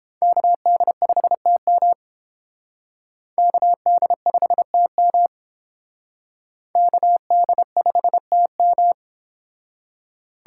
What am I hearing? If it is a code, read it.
Morse code, heard as a keyed tone, about 31 wpm, 700 Hz. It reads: KD5TM KD5TM KD5TM